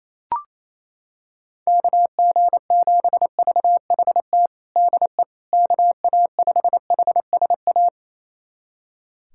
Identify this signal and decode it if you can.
Morse code, heard as a keyed tone, about 28 wpm, 700 Hz. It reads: KG7VHT DE KA5HSA
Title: USA Callsign DE Callsign - 6 Characters - 28wpm